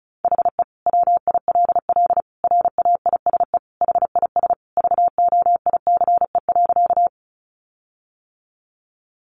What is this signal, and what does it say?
Morse code, heard as a keyed tone, about 34 wpm, 700 Hz. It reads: HE WILL RAISE HIS VOICE.
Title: Sentences from Top 1400 Words 35wpm